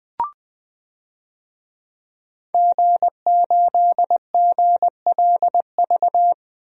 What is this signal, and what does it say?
Morse code, heard as a keyed tone, about 20 wpm, 700 Hz. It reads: G8GLV
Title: DXCC-100 Call Signs - 5 Characters - Repeated 3x - 20wpm